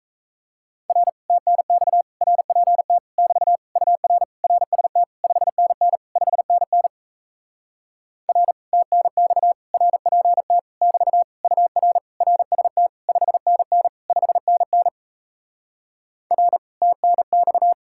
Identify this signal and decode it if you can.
Morse code, heard as a keyed tone, about 38 wpm, 700 Hz. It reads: R TNX RPT <BT> UR RST 5NN 5NN R TNX RPT <BT> UR RST 5NN 5NN R TNX